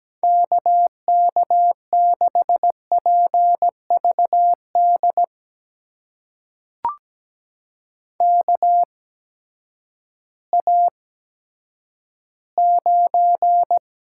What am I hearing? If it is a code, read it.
Morse code, heard as a keyed tone, about 17 wpm, spaced as for 10 wpm, 700 Hz. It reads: KK6PVD KA9
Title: USA Call Signs - 6 Characters - 8x Character Spacing - 17wpm